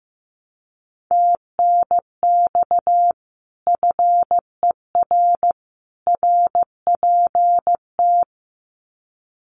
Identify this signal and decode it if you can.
Morse code, heard as a keyed tone, about 15 wpm, 700 Hz. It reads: TNX FER RPT
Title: Common QSO Phrases 15wpm